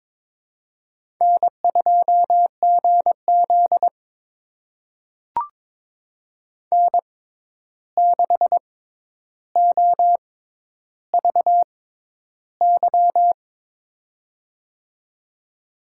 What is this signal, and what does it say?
Morse code, heard as a keyed tone, about 22 wpm, spaced as for 11 wpm, 700 Hz. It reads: N2GZ N6OVY
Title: USA Call Signs - 6x Character Spacing - 22wpm